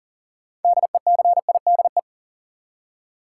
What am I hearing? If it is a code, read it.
Morse code, heard as a keyed tone, about 40 wpm, 700 Hz. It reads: DECIDE
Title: Six Letter Words - 40wpm